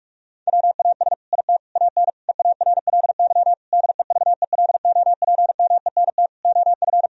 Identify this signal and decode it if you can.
Morse code, heard as a keyed tone, about 44 wpm, 700 Hz. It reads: WAS IT AN EARLY DEVELOPMENT OF
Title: Sentences from Top 1000 Words - Review Part 4 - 45wpm